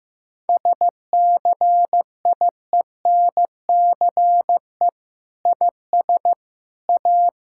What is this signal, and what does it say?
Morse code, heard as a keyed tone, about 15 wpm, 700 Hz. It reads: SCIENCE IS A
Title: Sentences from Top 1000 Words - Review Part 6 - 15wpm